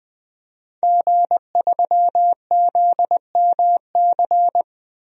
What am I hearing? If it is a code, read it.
Morse code, heard as a keyed tone, about 20 wpm, 700 Hz. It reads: G3ZMC
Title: DXCC-100 Call Signs - 5 Characters - 20wpm